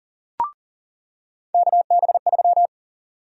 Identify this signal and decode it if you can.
Morse code, heard as a keyed tone, about 40 wpm, 700 Hz. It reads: KB3
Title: USA Call Sign Prefixes - 40wpm